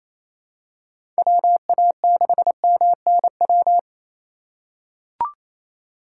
Morse code, heard as a keyed tone, about 28 wpm, 700 Hz. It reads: WA6MNW
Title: USA Call Signs - 3x Character Spacing - 28wpm